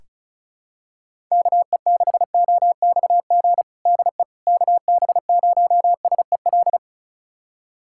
Morse code, heard as a keyed tone, about 36 wpm, 700 Hz. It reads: KE6OXG DE KB0SEL
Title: USA Callsign DE Callsign - 6 Characters - 35wpm